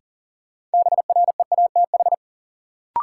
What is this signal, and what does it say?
Morse code, heard as a keyed tone, about 40 wpm, 700 Hz. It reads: BREATH
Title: Sets of 5 Words Encore - New Word Drill - 40wpm